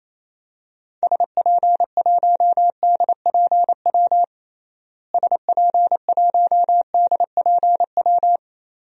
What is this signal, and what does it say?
Morse code, heard as a keyed tone, about 28 wpm, 700 Hz. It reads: SP1DPW SP1DPW
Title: DXCC-100 Call Signs - 6 Characters - Repeated 2x - 28wpm